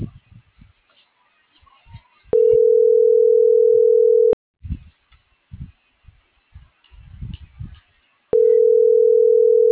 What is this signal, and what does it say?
Ringback tone (North America), 440 + 480 Hz.